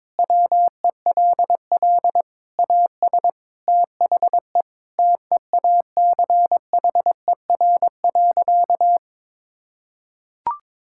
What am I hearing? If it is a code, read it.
Morse code, heard as a keyed tone, about 22 wpm, 700 Hz. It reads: WELL AS THE TEACHER.
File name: practice / 22wpm / Sentences with 4-Word Phrases - 22wpm